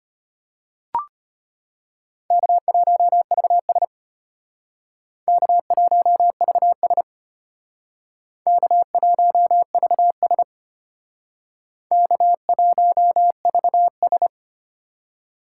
Morse code, heard as a keyed tone, about 33 wpm, 700 Hz. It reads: K1VS K1VS K1VS K1VS